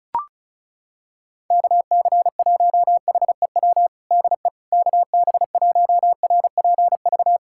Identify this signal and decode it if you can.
Morse code, heard as a keyed tone, about 34 wpm, 690 Hz. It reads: KC1HEW DE KB1RPV
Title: USA Callsign DE Callsign - 6 Characters - 35wpm